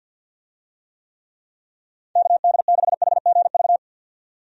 Morse code, heard as a keyed tone, about 51 wpm, 700 Hz. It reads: KD6HGV
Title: USA Call Signs - 6 Characters - 50wpm